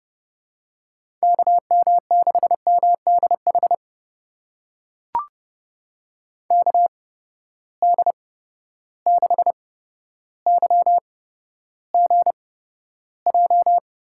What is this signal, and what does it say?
Morse code, heard as a keyed tone, about 31 wpm, spaced as for 13 wpm, 700 Hz. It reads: KM6MDH KD6YGJ
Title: USA Call Signs - 6 Characters - 8x Character Spacing - 30wpm